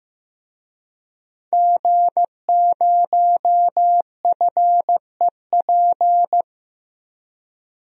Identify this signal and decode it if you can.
Morse code, heard as a keyed tone, about 15 wpm, 700 Hz. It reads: G0FEP
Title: DXCC-100 Call Signs - 5 Characters - 15wpm